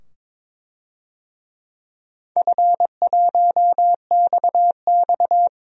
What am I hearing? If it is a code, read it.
Morse code, heard as a keyed tone, about 22 wpm, 700 Hz. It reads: F1XX
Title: DXCC-100 Call Signs - 4 Characters - 22wpm